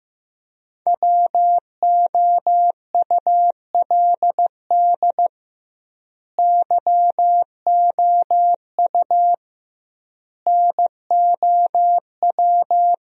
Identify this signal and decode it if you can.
Morse code, heard as a keyed tone, about 15 wpm, 700 Hz. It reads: WOULD YOU NOW
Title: 2x Word-Spacing - Sentences from Top 200 Words - 15wpm